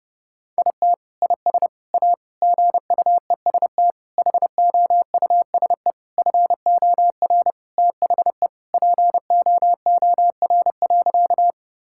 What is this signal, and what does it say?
Morse code, heard as a keyed tone, about 30 wpm, 700 Hz. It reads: IT IS A GUEST HOUSE FOR THE POOR.
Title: Sentences from Top 1600 Words 30wpm